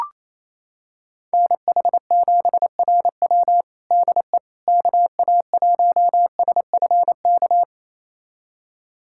Morse code, heard as a keyed tone, about 27 wpm, 700 Hz. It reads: NH7RW DE KA1SFK
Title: USA Callsign DE Callsign 28wpm